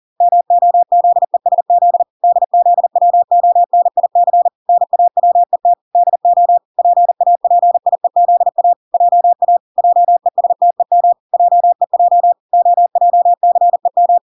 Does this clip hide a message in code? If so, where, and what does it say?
Yes: Morse code from 0.2 to 14.2 s, heard as a keyed tone, about 40 wpm, 700 Hz. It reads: MOZESZ DZWONIC NAWET DO PAPIEZA JA JESTEM JEJ OJCEM